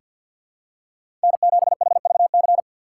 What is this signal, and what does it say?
Morse code, heard as a keyed tone, about 46 wpm, 700 Hz. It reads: N7HVC
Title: USA Call Signs - 5 Characters - 6x Character Spacing - 50wpm